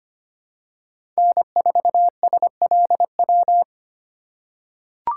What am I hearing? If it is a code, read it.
Morse code, heard as a keyed tone, about 25 wpm, 700 Hz. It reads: N4SLW
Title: USA Call Signs - 2x Character Spacing - 25wpm